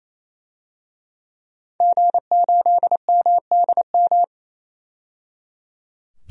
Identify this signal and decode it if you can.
Morse code, heard as a keyed tone, about 27 wpm, 700 Hz. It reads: G8MDM
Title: DXCC-100 Call Signs - 5 Characters - 28wpm